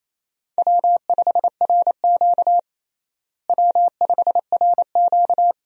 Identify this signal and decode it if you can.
Morse code, heard as a keyed tone, about 28 wpm, 700 Hz. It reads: W5RQ W5RQ
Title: USA Call Signs - 4 Characters - Repeated 2x - 28wpm